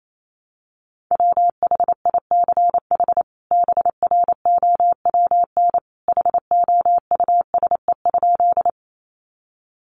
Morse code, heard as a keyed tone, about 28 wpm, 700 Hz. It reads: WHICH BROWN HOUSE?
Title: Sentences from Top 1200 Words 28wpm